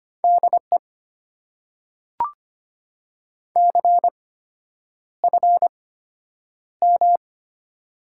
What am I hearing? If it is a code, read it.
Morse code, heard as a keyed tone, about 25 wpm, spaced as for 9 wpm, 690 Hz. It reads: DE CFM
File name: Top 90 QSO Elements - 8x Character Spacing - 25wpm